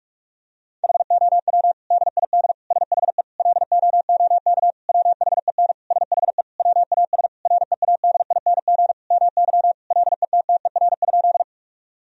Morse code, heard as a keyed tone, about 46 wpm, 700 Hz. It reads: HOW DID SHE LOOK WHEN SHE WAS READING MY LETTER?
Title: Sentences from Top 1000 Words - Review Part 3 - 45wpm